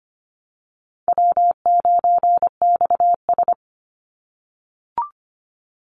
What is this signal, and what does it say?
Morse code, heard as a keyed tone, about 25 wpm, 700 Hz. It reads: W9XS
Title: Call Signs - 25wpm